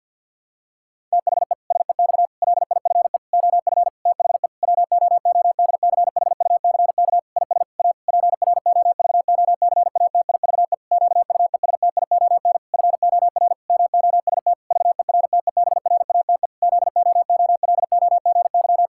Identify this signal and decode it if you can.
Morse code, heard as a keyed tone, about 50 wpm, 700 Hz. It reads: THE SEX LIFE OF THE WOODCHUCK IS A PROVOCATIVE QUESTION FOR MOST VERTEBRATE ZOOLOGY
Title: Pangram Challenge v2 - 50wpm